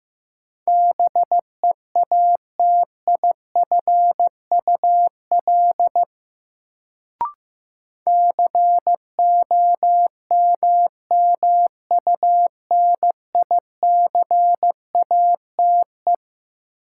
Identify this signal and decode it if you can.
Morse code, heard as a keyed tone, about 16 wpm, 700 Hz. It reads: BEATIFUL COMMUNICATE
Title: Sentences from Top 1300 Words - New Word Drill - 15wpm